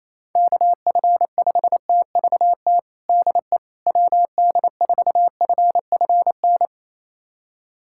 Morse code, heard as a keyed tone, about 28 wpm, 700 Hz. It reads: KF5TVT DE WD4FFN
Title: USA Callsign DE Callsign - 6 Characters - 28wpm